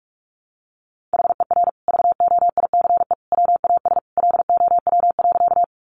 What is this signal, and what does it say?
Morse code, heard as a keyed tone, about 44 wpm, 700 Hz. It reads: HER VOICE WAS LOW.
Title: Sentences from Top 500 Words 45wpm